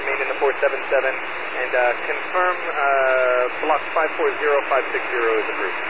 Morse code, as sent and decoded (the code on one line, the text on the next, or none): none